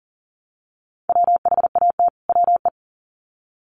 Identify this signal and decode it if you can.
Morse code, heard as a keyed tone, about 40 wpm, 700 Hz. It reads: WHAT WE
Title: Sets of 2 Words 40wpm